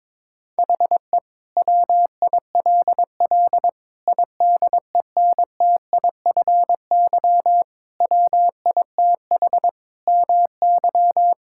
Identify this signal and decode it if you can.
Morse code, heard as a keyed tone, about 22 wpm, 700 Hz. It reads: HE WILL IDENTIFY WITH MY
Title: Sentences from Top 1000 Words - Review Part 6 - 22wpm